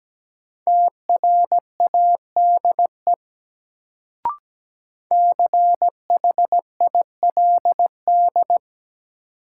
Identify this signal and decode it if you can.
Morse code, heard as a keyed tone, about 17 wpm, 700 Hz. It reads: TRADE CHILD
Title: Sets of 5 Words Encore - New Word Drill - 17wpm